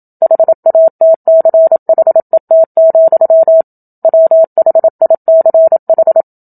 Morse code, heard as a keyed tone, about 27 wpm, 650 Hz. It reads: HATCHET, WHICH
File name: Book Two / 01-Chapter 1 - 27 WPM